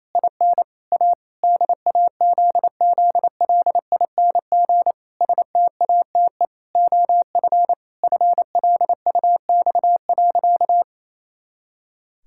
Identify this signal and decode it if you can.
Morse code, heard as a keyed tone, about 27 wpm, 700 Hz. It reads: IN A DAZZLING STATE OF FLUX.
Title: Pangram Challenge v2 - 28wpm